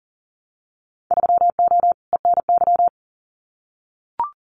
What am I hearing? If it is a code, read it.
Morse code, heard as a keyed tone, about 40 wpm, 700 Hz. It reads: 3O ENY
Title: Winter Field Day Exchanges Encore 40wpm